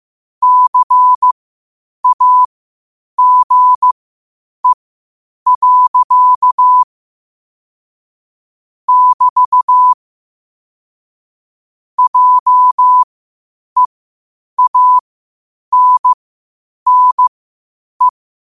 Morse code, heard as a keyed tone, about 15 wpm, spaced as for 8 wpm, 1000 Hz. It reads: CAGE. <BT> JEANNE
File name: qotdfiles / 8wpm